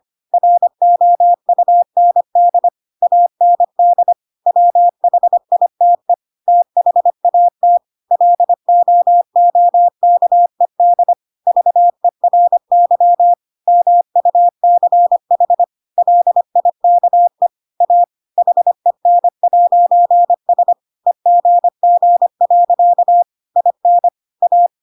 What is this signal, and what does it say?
Morse code, heard as a keyed tone, about 25 wpm, 700 Hz. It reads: ROUND AND WHITE THAT LOOKED VERY MUCH LIKE A HEN'S EGG. IN A
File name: Pinocchio / Pinocchio ch5 25 25 ik1hge